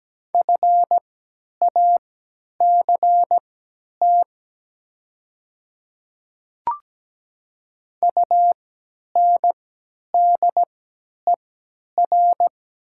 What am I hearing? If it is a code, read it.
Morse code, heard as a keyed tone, about 17 wpm, spaced as for 10 wpm, 700 Hz. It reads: FACT UNDER
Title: 3x Character Spacing - Top 500 Words - 17wpm